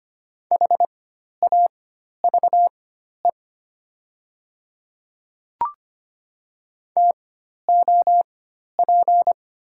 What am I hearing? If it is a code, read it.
Morse code, heard as a keyed tone, about 25 wpm, spaced as for 12 wpm, 700 Hz. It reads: HAVE TOP
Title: 4x Character Spacing - Top 500 Words - 25wpm